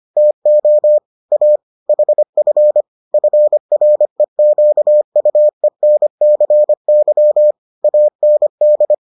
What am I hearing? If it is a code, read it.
Morse code, heard as a keyed tone, about 25 wpm, 600 Hz. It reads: TO A HF FREQUENCY AND